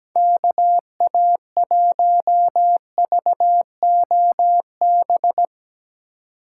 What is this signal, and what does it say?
Morse code, heard as a keyed tone, about 20 wpm, 700 Hz. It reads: KA1VOB
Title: Speed Racing - USA 2x3 Call Signs 17wpm